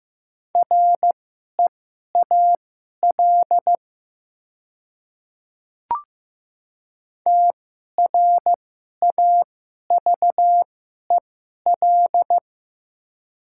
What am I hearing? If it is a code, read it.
Morse code, heard as a keyed tone, about 15 wpm, spaced as for 11 wpm, 700 Hz. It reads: REAL TRAVEL
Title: 2x Character Spacing - Top 500 Words - 15wpm